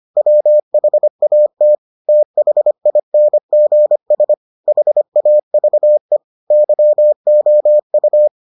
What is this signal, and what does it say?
Morse code, heard as a keyed tone, about 25 wpm, 600 Hz. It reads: WHAT THINGS HAVE YOU